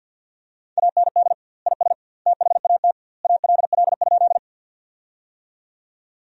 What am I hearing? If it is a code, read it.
Morse code, heard as a keyed tone, about 50 wpm, 700 Hz. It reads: AND IS THAT ALL?
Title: 2x Word-Spacing - Sentences from Top 100 Words - 50wpm